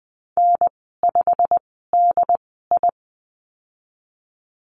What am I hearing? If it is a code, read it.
Morse code, heard as a keyed tone, about 19 wpm, spaced as for 14 wpm, 700 Hz. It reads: N5DI
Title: 207 Lesson 33g - Letter j - Call Signs 20wpm